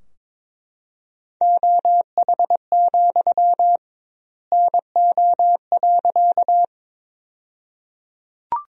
Morse code, heard as a keyed tone, about 22 wpm, 700 Hz. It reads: OH, NO.
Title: 2x Word-Spacing - Sentences from Top 500 Words - 22wpm